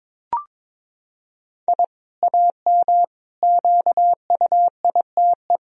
Morse code, heard as a keyed tone, about 21 wpm, 700 Hz. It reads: I AM QUITE
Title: Sentences from Top 1000 Words - Review Part 2 - 22wpm